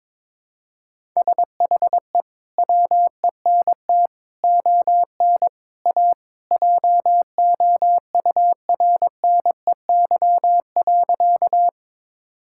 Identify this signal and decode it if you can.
Morse code, heard as a keyed tone, about 22 wpm, 700 Hz. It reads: SHE WENT ON A JOURNEY.
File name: practice / 22wpm / Sentences with 2-Word Phrases - 22wpm